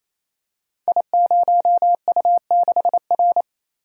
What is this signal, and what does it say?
Morse code, heard as a keyed tone, about 27 wpm, 700 Hz. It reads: I0U6R
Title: Code Groups - 5 Letter:Numbers - 28wpm